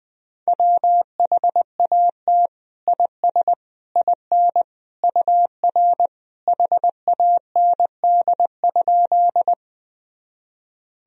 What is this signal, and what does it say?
Morse code, heard as a keyed tone, about 20 wpm, 700 Hz. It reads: WHAT IS IN UR HAND?
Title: Sentences from Top 200 Words and Abbreviations 20wpm